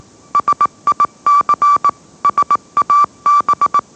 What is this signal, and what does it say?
Morse code, heard as a keyed tone, about 18 wpm, 1160 Hz. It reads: SIC SAB
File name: mrx / mrx081 20wpm